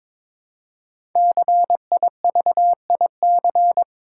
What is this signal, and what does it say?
Morse code, heard as a keyed tone, about 22 wpm, 700 Hz. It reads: CIVIC
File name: 5 Letter Words - First 3 Letters Given - 22wpm